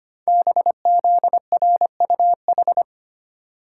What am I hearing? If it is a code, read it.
Morse code, heard as a keyed tone, about 25 wpm, 700 Hz. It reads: BZRUH
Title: Code Groups - 5 Letters - 25wpm